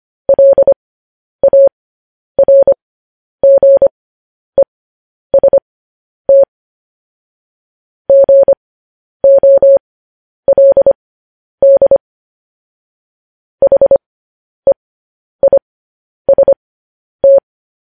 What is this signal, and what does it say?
Morse code, heard as a keyed tone, about 25 wpm, spaced as for 10 wpm, 550 Hz. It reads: LARGEST GOLD HEIST